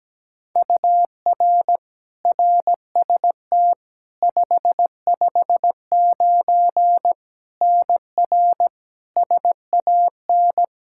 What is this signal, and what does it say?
Morse code, heard as a keyed tone, about 17 wpm, 700 Hz. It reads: UR RST 559 NR SAN
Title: Practice QSO - Exchange 1 - Beginning - 17wpm